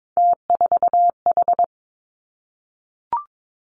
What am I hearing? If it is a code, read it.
Morse code, heard as a keyed tone, about 22 wpm, 700 Hz. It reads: T4H
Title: Three Challenging Letter-Number Combinations 22wpm